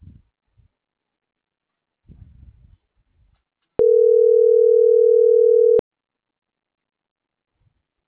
Ringback tone (North America), 440 + 480 Hz.